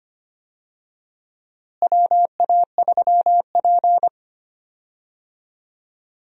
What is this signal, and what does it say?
Morse code, heard as a keyed tone, about 25 wpm, 700 Hz. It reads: WA3P
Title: USA Call Signs - 4 Characters - Repeated 3x - 25wpm